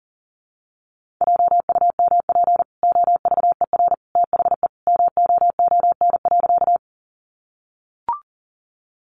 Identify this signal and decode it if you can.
Morse code, heard as a keyed tone, about 40 wpm, 700 Hz. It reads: JUMP OVER THE MOON.
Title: Sentences from Top 1300 Words 40wpm